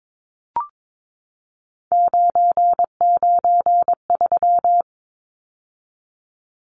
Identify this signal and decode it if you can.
Morse code, heard as a keyed tone, about 22 wpm, 700 Hz. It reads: 993
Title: Numbers - 3-Digit 22wpm